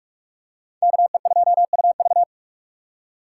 Morse code, heard as a keyed tone, about 44 wpm, 700 Hz. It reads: KE2UV
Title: USA Call Signs - 5 Characters - 2x Character Spacing - 45wpm